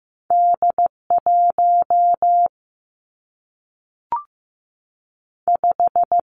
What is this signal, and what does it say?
Morse code, heard as a keyed tone, about 15 wpm, 700 Hz. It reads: D 1 5
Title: Two Challenging Letter-Number Combinations 15wpm